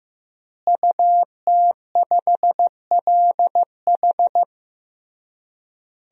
Morse code, heard as a keyed tone, about 15 wpm, 700 Hz. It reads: UT5LH
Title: DXCC-100 Call Signs - 5 Characters - Repeated 3x - 15wpm